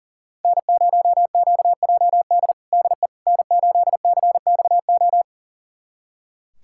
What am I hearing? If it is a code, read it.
Morse code, heard as a keyed tone, about 40 wpm, 700 Hz. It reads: N0QJD DE N8CXO
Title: USA Callsign DE Callsign - 5 Characters - 40wpm